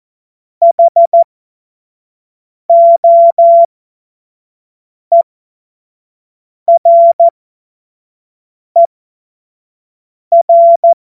Morse code, heard as a keyed tone, about 14 wpm, spaced as for 5 wpm, 680 Hz. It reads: HOERER
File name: Morsekurs / Witze / Witz 11. Juli 2023 25 F70